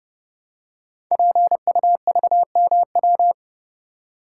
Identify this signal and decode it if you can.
Morse code, heard as a keyed tone, about 30 wpm, 700 Hz. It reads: PUVMW
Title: Code Groups - 5 Letters - 30wpm